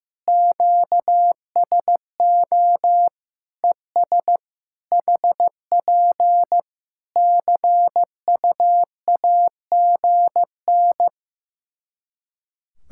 Morse code, heard as a keyed tone, about 15 wpm, 700 Hz. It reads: QSO ES HP CUAGN
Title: Practice QSO - Exchange 4 - Ending - 15wpm